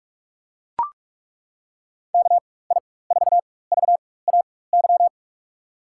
Morse code, heard as a keyed tone, about 42 wpm, spaced as for 21 wpm, 700 Hz. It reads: KI4VAY
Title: USA Call Signs - 6 Characters - 4x Character Spacing - 45wpm